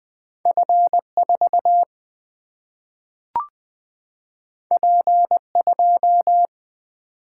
Morse code, heard as a keyed tone, about 20 wpm, 700 Hz. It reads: F 4 P 2
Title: Single-Letter Single-Number 20wpm